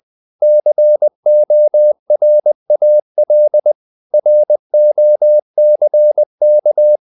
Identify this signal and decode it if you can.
Morse code, heard as a keyed tone, about 20 wpm, 600 Hz. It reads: CORAL ROCK
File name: the old man and the sea part 2 20wpm